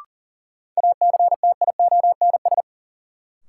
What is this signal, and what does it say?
Morse code, heard as a keyed tone, about 40 wpm, 700 Hz. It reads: ACTIONS
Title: Sentences from Top 1000 Words - Review Part 6 - New Word Drill - 40wpm